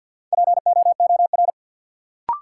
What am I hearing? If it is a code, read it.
Morse code, heard as a keyed tone, about 50 wpm, 700 Hz. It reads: POOR
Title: Sentences from Top 1000 Words - New Word Drill - 50wpm